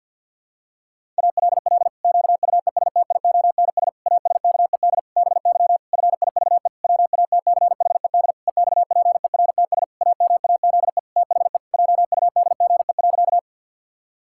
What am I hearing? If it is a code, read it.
Morse code, heard as a keyed tone, about 52 wpm, 700 Hz. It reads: ALL QUESTIONS ASKED BY FIVE WATCHED EXPERTS AMAZE THE JUDGE.
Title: Pangram Challenge v2 - 50wpm